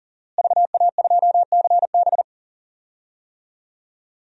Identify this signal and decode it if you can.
Morse code, heard as a keyed tone, about 37 wpm, 700 Hz. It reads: VA2CB
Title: DXCC-100 Call Signs - 5 Characters - Repeated 2x - 40wpm